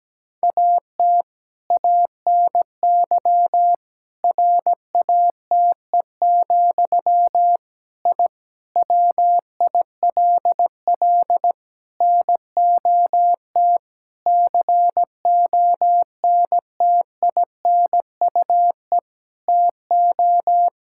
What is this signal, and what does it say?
Morse code, heard as a keyed tone, about 17 wpm, 700 Hz. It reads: AT ANY RATE, I WILL NOT CONTINUE TO
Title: Sentences from Top 1000 Words - Review Part 6 - 17wpm